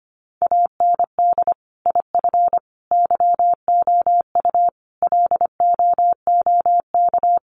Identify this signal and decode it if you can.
Morse code, heard as a keyed tone, about 25 wpm, 700 Hz. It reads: AND IF YOU LOOK
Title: Sets of 5 Words Encore2 25wpm